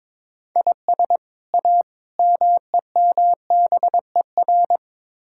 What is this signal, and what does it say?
Morse code, heard as a keyed tone, about 22 wpm, 700 Hz. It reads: IS A MEMBER
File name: practice / 22wpm / Build A Phrase - 5-Word Phrases - 22wpm